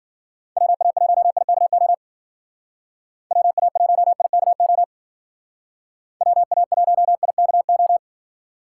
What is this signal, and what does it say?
Morse code, heard as a keyed tone, about 50 wpm, 700 Hz. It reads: WA1IKO WA1IKO WA1IKO
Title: Speed Racing - USA 2x3 Call Signs 40wpm